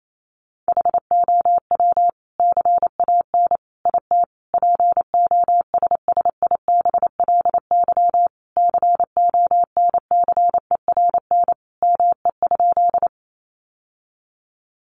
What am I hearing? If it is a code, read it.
Morse code, heard as a keyed tone, about 28 wpm, 700 Hz. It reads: HOW CAN IT POSSIBLY CONCERN ME?
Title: Sentences from Top 1300 Words 28wpm